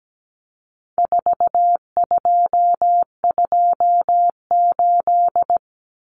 Morse code, heard as a keyed tone, about 17 wpm, 700 Hz. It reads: 4228
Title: Numbers - 4-Digit 17wpm